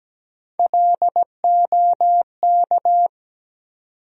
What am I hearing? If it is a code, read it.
Morse code, heard as a keyed tone, about 17 wpm, 700 Hz. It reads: LOK
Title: USA Call Sign Postfixes - 3 Letters 17wpm